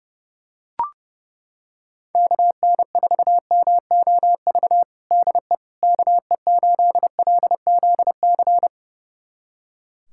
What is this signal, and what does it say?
Morse code, heard as a keyed tone, about 32 wpm, 700 Hz. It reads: KN4MOV DE KE8LZC
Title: USA Callsign DE Callsign - 6 Characters - 30wpm